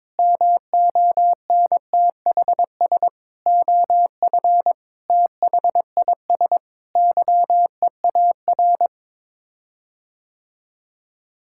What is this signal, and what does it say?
Morse code, heard as a keyed tone, about 22 wpm, 700 Hz. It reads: MONTHS OF THIS YEAR
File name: practice / 22wpm / Sets of 8 Words - 22wpm